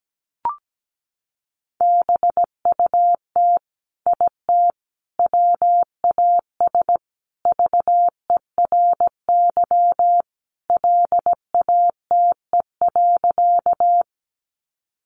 Morse code, heard as a keyed tone, about 17 wpm, 700 Hz. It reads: BUT IT WAS VERY LATE.
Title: Sentences from Top 300 Words 17wpm